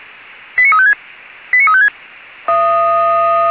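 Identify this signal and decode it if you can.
Five-tone selective call (ZVEI), heard as quick runs of five tones, tones of about 70 ms. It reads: 89356 89356
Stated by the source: ZVEI five-tone selective call received by SDR on the Feuerwehr Orlamünde (fire brigade) audio server.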